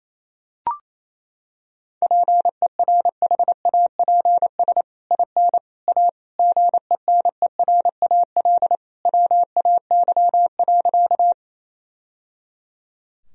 Morse code, heard as a keyed tone, about 28 wpm, 700 Hz. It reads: PERHAPS IN A GENERAL WAY.
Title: Sentences from Top 1000 Words - Review Part 3 - 28wpm